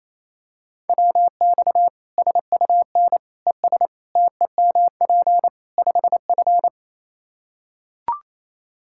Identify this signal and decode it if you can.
Morse code, heard as a keyed tone, about 28 wpm, 700 Hz. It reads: WX SUN ES TEMP 5F
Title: Practice QSO - Exchange 2 - Info - 28wpm